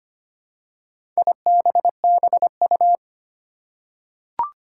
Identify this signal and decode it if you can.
Morse code, heard as a keyed tone, about 25 wpm, 700 Hz. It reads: IBBU
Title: Four Challenging Letter-Number Combinations 25wpm